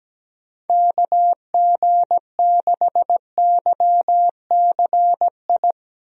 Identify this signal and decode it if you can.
Morse code, heard as a keyed tone, about 17 wpm, 700 Hz. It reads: KG6YCI
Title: USA Call Signs - 6 Characters - Repeated 3x - 17wpm